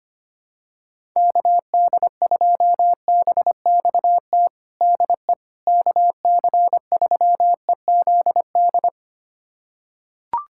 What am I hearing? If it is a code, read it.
Morse code, heard as a keyed tone, about 25 wpm, 700 Hz. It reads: KD2BXT DE KC3EZD
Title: USA Callsign DE Callsign 25wpm